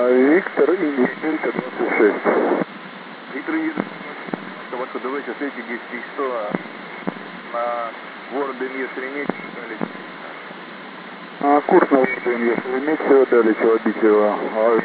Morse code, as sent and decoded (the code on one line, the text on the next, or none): none